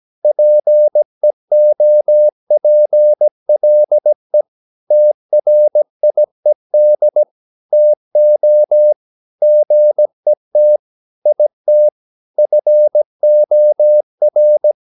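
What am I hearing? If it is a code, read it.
Morse code, heard as a keyed tone, about 17 wpm, 600 Hz. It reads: PEOPLE TRIED TO GET IT FOR
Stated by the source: 17 wpm Morse practice sending the English text of a novel.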